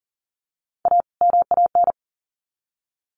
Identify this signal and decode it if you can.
Morse code, heard as a keyed tone, about 40 wpm, 700 Hz. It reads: A MAN
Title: Sets of 2 Words 40wpm